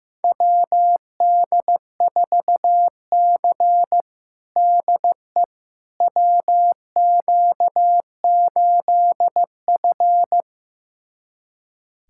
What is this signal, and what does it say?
Morse code, heard as a keyed tone, about 15 wpm, 700 Hz. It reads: WD4C DE WQ8F
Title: USA Callsign DE Callsign - 4 Characters - 15wpm